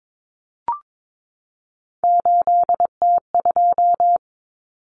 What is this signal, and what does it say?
Morse code, heard as a keyed tone, about 22 wpm, 700 Hz. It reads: 8T2
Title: Common Cut Numbers - 3-Digit 22wpm